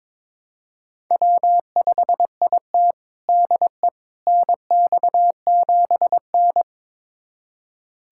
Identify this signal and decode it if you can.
Morse code, heard as a keyed tone, about 22 wpm, 700 Hz. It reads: W5IT DE NX7N
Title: USA Callsign DE Callsign - 4 Characters - 22wpm